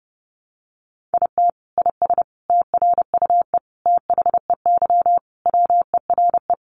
Morse code, heard as a keyed tone, about 28 wpm, 700 Hz. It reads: IT IS TRUE THEY WERE
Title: Sentences from Top 1400 Words 30wpm